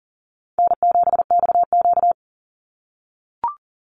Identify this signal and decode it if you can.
Morse code, heard as a keyed tone, about 40 wpm, 700 Hz. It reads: N7XQ
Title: USA Call Signs 40wpm